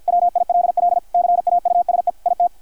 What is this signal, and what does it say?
Morse code, heard as a keyed tone, about 48 wpm, 660 Hz, with radio noise. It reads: WILL CAUSE IT
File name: archived / CRSnet